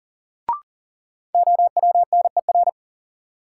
Morse code, heard as a keyed tone, about 40 wpm, 700 Hz. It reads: OWNER
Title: Sentences from Top 1600 Words - New Word Drill - 40wpm